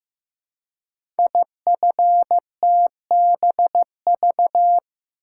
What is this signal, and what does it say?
Morse code, heard as a keyed tone, about 15 wpm, 700 Hz. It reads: IFTBV
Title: Code Groups - 5 Letters - 15wpm